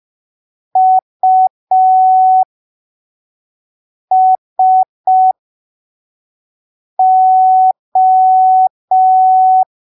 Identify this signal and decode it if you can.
Morse code, heard as a keyed tone, about 5 wpm, spaced as for 3 wpm, 750 Hz. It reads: USO
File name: arquivos / texto / texto 03 5ppm